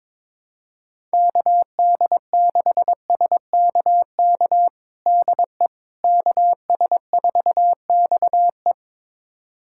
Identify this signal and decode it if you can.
Morse code, heard as a keyed tone, about 22 wpm, 700 Hz. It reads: KD6SKK DE KS4XE
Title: USA Callsign DE Callsign 22wpm